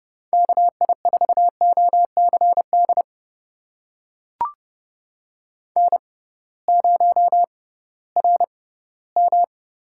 Morse code, heard as a keyed tone, about 30 wpm, spaced as for 17 wpm, 700 Hz. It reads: KI4OCD N0RM
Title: USA Call Signs - 6x Character Spacing - 30wpm